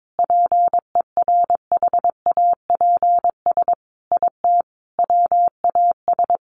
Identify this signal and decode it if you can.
Morse code, heard as a keyed tone, about 22 wpm, 700 Hz. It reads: PERHAPS IT WAS
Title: Sentences from Top 500 Words 22wpm